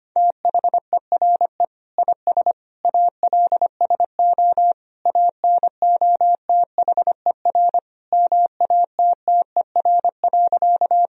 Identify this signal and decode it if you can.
Morse code, heard as a keyed tone, about 25 wpm, 700 Hz. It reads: THERE IS ALSO ANOTHER MATTER.
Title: Sentences from Top 1000 Words - Review Part 1 - 25wpm